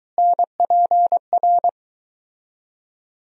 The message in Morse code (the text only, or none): NPR